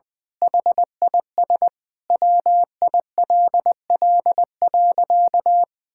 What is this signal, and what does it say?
Morse code, heard as a keyed tone, about 20 wpm, 700 Hz. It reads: HIS WILL.
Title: Sentences with 9-Word Phrases - 20wpm